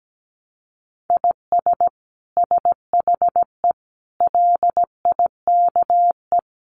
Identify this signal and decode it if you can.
Morse code, heard as a keyed tone, about 17 wpm, 700 Hz. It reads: IS SHE LIKE
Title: Sentences from Top 100 Words 17wpm